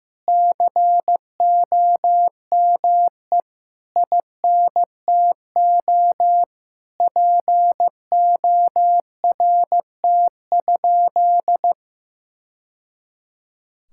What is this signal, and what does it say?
Morse code, heard as a keyed tone, about 15 wpm, 700 Hz. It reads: COME INTO PORT?
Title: Sentences from Top 1000 Words - Review Part 5 - 15wpm